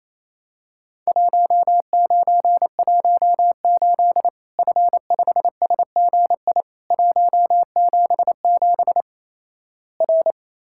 Morse code, heard as a keyed tone, about 28 wpm, 700 Hz. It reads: 1918 F5SGI 177 R
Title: Worked All Europe DX Contest - QTC Exchange - 28wpm